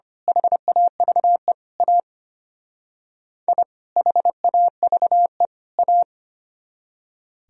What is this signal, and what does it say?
Morse code, heard as a keyed tone, about 29 wpm, 700 Hz. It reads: HAVE A I HAVE A
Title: Speed-Racing - Sets of 3 Words 25wpm